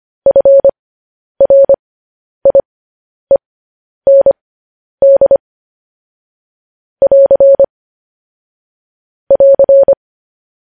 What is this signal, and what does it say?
Morse code, heard as a keyed tone, about 25 wpm, spaced as for 10 wpm, 550 Hz. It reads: FRIEND <AR> <AR>